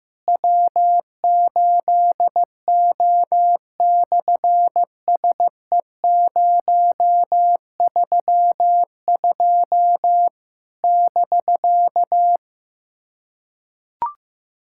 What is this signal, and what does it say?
Morse code, heard as a keyed tone, about 15 wpm, 700 Hz. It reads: W8O/SE032 <BK>
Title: SOTA QSO Practice 15wpm